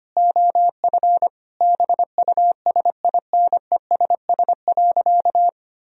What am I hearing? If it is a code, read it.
Morse code, heard as a keyed tone, about 25 wpm, 700 Hz. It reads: OF BUSINESS.
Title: Sentences from Top 1000 Words - Review Part 3 - 25wpm